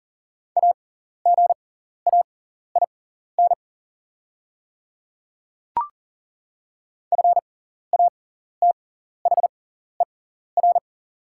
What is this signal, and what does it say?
Morse code, heard as a keyed tone, about 40 wpm, spaced as for 14 wpm, 690 Hz. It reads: AGAIN FATHER
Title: 6x Character Spacing - Top 200 Words - 40wpm